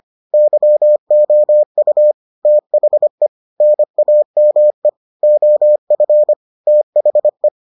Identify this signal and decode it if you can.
Morse code, heard as a keyed tone, about 25 wpm, 600 Hz. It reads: YOU THE NAME OF THE